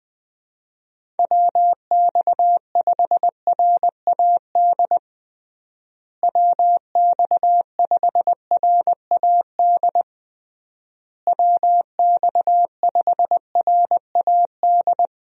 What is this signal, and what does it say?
Morse code, heard as a keyed tone, about 20 wpm, 700 Hz. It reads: WX5RAD WX5RAD WX5RAD
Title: USA Call Signs - 6 Characters - Repeated 3x - 20wpm